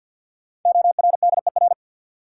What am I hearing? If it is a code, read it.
Morse code, heard as a keyed tone, about 53 wpm, 700 Hz. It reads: ORDER
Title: Sets of 2 Words - New Word Drill - 50wpm